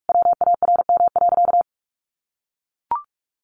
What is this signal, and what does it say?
Morse code, heard as a keyed tone, about 44 wpm, 700 Hz. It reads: WARM.
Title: Sentences from Top 1500 Words 45wpm